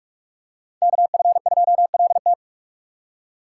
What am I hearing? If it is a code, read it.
Morse code, heard as a keyed tone, about 46 wpm, 700 Hz. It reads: KF2LT